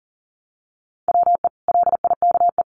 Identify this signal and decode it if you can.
Morse code, heard as a keyed tone, about 40 wpm, 700 Hz. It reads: WE LIKE
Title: Sentences from Top 1100 Words 40wpm